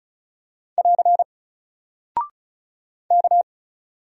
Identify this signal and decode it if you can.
Morse code, heard as a keyed tone, about 36 wpm, 700 Hz. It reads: <AR> K
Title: Single Character (Letter:Number:Punctuation:Prosign) 35wpm